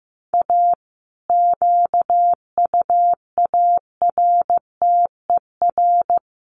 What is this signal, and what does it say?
Morse code, heard as a keyed tone, about 15 wpm, 700 Hz. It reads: A QUARTER
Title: Sentences from Top 1300 Words 15wpm